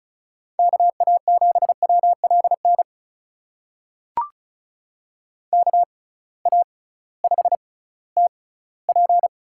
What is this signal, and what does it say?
Morse code, heard as a keyed tone, about 36 wpm, spaced as for 19 wpm, 690 Hz. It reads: KA7WLN KA5TP
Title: USA Call Signs - 6 Characters - 6x Character Spacing - 35wpm